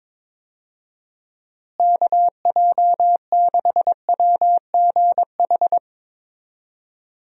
Morse code, heard as a keyed tone, about 22 wpm, 700 Hz. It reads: KJ6WGH
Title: USA Call Signs - 6 Characters - Repeated 2x - 22wpm